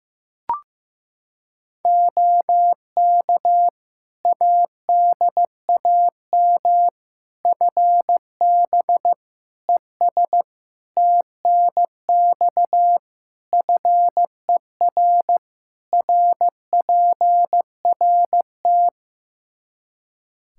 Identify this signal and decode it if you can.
Morse code, heard as a keyed tone, about 15 wpm, 700 Hz. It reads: OK ADAM FB ES TNX FER RPRT
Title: Practice QSO - Exchange 2 - Info - 15wpm